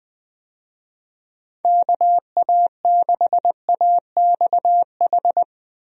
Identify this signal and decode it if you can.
Morse code, heard as a keyed tone, about 21 wpm, 700 Hz. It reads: KA6AXH
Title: USA Call Signs - 6 Characters - Repeated 3x - 20wpm